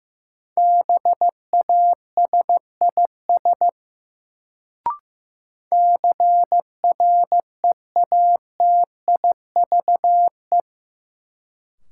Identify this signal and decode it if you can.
Morse code, heard as a keyed tone, about 15 wpm, 700 Hz. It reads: BASIS CREATIVE
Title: Sentences from Top 1000 Words - Review Part 5 - New Word Drill - 15wpm